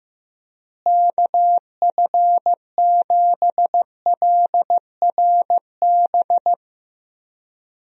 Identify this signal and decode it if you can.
Morse code, heard as a keyed tone, about 15 wpm, 700 Hz. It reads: KF7LRB
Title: USA Call Signs - 6 Characters - Repeated 2x - 15wpm